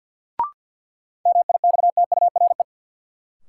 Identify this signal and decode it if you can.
Morse code, heard as a keyed tone, about 48 wpm, 700 Hz. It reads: MIXTURE
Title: Sentences from Top 1500 Words - New Word Drill - 50wpm